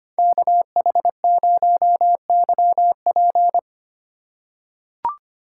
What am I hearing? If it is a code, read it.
Morse code, heard as a keyed tone, about 25 wpm, 700 Hz. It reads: KH0YP
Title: USA Call Signs - 3x Character Spacing - 25wpm